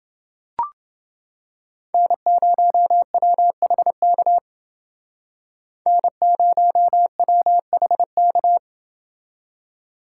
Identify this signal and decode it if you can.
Morse code, heard as a keyed tone, about 29 wpm, 700 Hz. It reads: N0WHK N0WHK